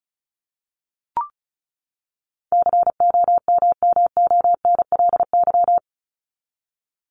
Morse code, heard as a keyed tone, about 34 wpm, 700 Hz. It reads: COMMONLY
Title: LY Suffix - Top 100 Words with Base Words 35wpm